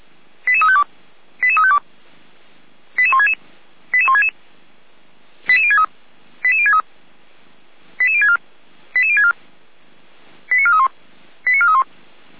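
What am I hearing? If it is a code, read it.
Five-tone selective call (ZVEI), heard as quick runs of five tones, tones of about 70 ms. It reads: 88352 88352 88166 88166 88063 88063 88064 88064 89431 89431
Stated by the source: ZVEI five-tone selective call received by SDR, Orlamünde fire brigade (feuerwehr-orlamuende).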